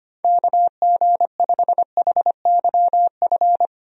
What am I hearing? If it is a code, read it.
Morse code, heard as a keyed tone, about 25 wpm, 700 Hz. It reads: KG5HYF
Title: USA Call Signs - 6 Characters - Repeated 3x - 25wpm